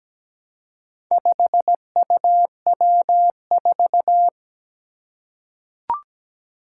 Morse code, heard as a keyed tone, about 17 wpm, 700 Hz. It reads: 5UW4
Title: Four Challenging Letter-Number Combinations 17wpm